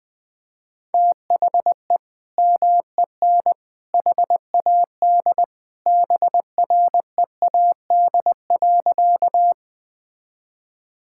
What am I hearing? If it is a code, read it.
Morse code, heard as a keyed tone, about 21 wpm, 700 Hz. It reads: THE MEN HAD BREAD.
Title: Sentences from Top 1600 Words 20wpm